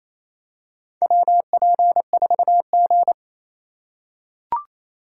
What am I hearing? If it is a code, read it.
Morse code, heard as a keyed tone, about 28 wpm, 700 Hz. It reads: WP4G
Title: USA Call Signs - 4 Characters - 2x Character Spacing - 28wpm